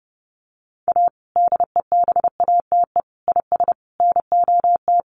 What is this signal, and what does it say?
Morse code, heard as a keyed tone, about 30 wpm, 700 Hz. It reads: A DEBATE IS NOT
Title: Sentences from Top 1100 Words 30wpm